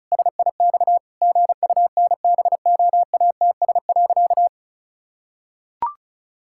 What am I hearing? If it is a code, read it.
Morse code, heard as a keyed tone, about 36 wpm, 700 Hz. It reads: SIX GUNBOATS.
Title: Pangram Challenge v2 - 35wpm